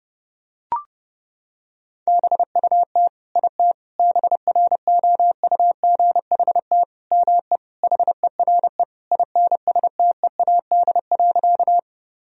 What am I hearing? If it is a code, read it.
Morse code, heard as a keyed tone, about 32 wpm, 700 Hz. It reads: BUT IT BROUGHT ME HERE INSTEAD.
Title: Sentences from Top 1000 Words - Review Part 4 - 30wpm